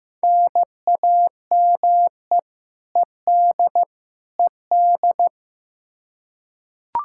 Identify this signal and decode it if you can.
Morse code, heard as a keyed tone, about 15 wpm, 700 Hz. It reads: NAME ED ED
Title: Practice QSO - Exchange 1 - Beginning - 15wpm